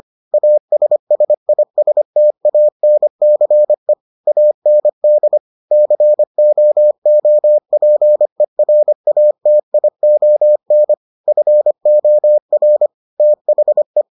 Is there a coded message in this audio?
Yes: Morse code from 0.3 to 14.0 s, heard as a keyed tone, about 25 wpm, 600 Hz. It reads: ASSISTANCE AND COOPERATION FOR THE